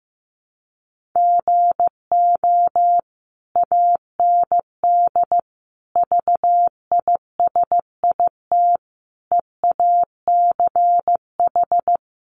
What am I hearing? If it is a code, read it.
Morse code, heard as a keyed tone, about 15 wpm, 700 Hz. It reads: GO AND VISIT EACH
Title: Sentences from Top 1100 Words 15wpm